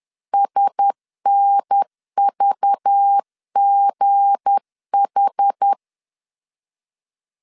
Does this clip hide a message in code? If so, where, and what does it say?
Yes: Morse code from 0.3 to 5.7 s, heard as a keyed tone, about 10 wpm, 800 Hz. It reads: SNVGH